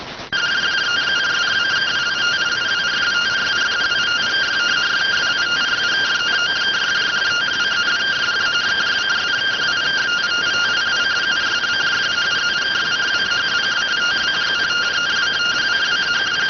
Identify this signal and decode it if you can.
Two-tone teleprinter signal, about 170 Hz shift.